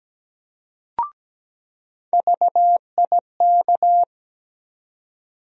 Morse code, heard as a keyed tone, about 17 wpm, 700 Hz. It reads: VIK
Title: USA Call Sign Postfixes - 3 Letters 17wpm